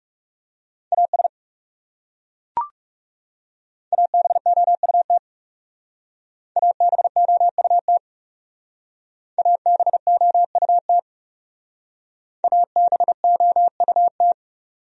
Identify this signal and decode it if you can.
Morse code, heard as a keyed tone, about 38 wpm, 700 Hz. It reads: AS ABOUT ABOUT ABOUT ABOUT